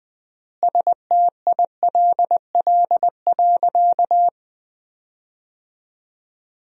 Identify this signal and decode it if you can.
Morse code, heard as a keyed tone, about 21 wpm, 700 Hz. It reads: STILL.
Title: 3x Word-Spacing - Sentences from Top 400 Words - 20wpm